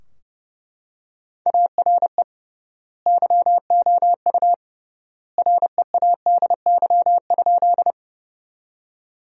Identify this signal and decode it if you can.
Morse code, heard as a keyed tone, about 30 wpm, 700 Hz. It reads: ARE YOU READY?
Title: 3x Word-Spacing - Sentences from Top 400 Words - 30wpm